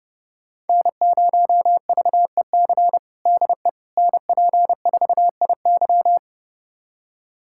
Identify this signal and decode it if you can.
Morse code, heard as a keyed tone, about 30 wpm, 700 Hz. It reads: N0VEC DE NP4IY
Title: USA Callsign DE Callsign - 5 Characters - 30wpm